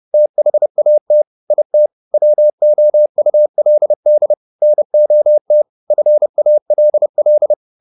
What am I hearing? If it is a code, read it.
Morse code, heard as a keyed tone, about 30 wpm, 600 Hz. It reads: THAT IT WOULD NOT FALL